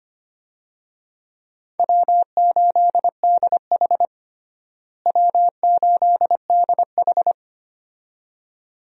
Morse code, heard as a keyed tone, about 25 wpm, 700 Hz. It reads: W8DH W8DH
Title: USA Call Signs - 4 Characters - Repeated 2x - 25wpm